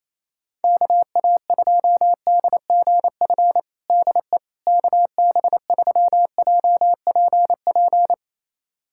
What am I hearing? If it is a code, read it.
Morse code, heard as a keyed tone, about 28 wpm, 700 Hz. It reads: KA2DGF DE KB3JPP
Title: USA Callsign DE Callsign - 6 Characters - 28wpm